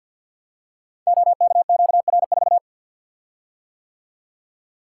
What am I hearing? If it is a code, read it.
Morse code, heard as a keyed tone, about 48 wpm, 700 Hz. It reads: OKQR4
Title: Code Groups - 5 Letter:Numbers - 50wpm